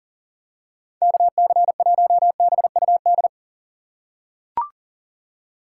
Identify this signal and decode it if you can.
Morse code, heard as a keyed tone, about 38 wpm, 690 Hz. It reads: KC1BUD
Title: USA Call Signs - 6 Characters - 2x Character Spacing - 40wpm